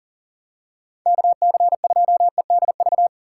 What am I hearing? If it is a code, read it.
Morse code, heard as a keyed tone, about 40 wpm, 700 Hz. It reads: KC2EDV
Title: USA Call Signs - 6 Characters - 6x Character Spacing - 40wpm